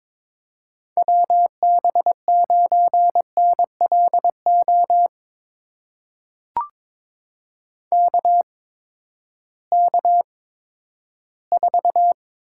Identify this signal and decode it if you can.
Morse code, heard as a keyed tone, about 22 wpm, spaced as for 13 wpm, 700 Hz. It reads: WB9NLO KK4
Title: USA Call Signs - 6 Characters - 8x Character Spacing - 22wpm